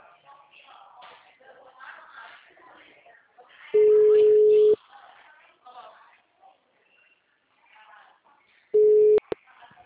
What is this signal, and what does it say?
Ringback tone (Europe, 425 Hz), with speech.